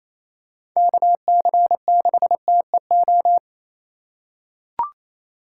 Morse code, heard as a keyed tone, about 28 wpm, 700 Hz. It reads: KC6TEO
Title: USA Call Signs - 4x Character Spacing - 28wpm